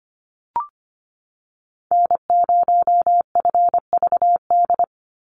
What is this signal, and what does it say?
Morse code, heard as a keyed tone, about 25 wpm, 700 Hz. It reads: N0FVD
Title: USA Call Signs 25wpm